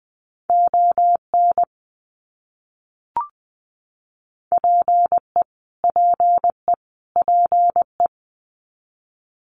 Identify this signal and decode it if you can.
Morse code, heard as a keyed tone, about 20 wpm, 700 Hz. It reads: ON PE PE PE
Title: Canadian Provinces and Territories - Repeated 3x - 20wpm